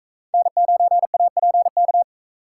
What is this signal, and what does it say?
Morse code, heard as a keyed tone, about 42 wpm, 700 Hz. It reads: N9APK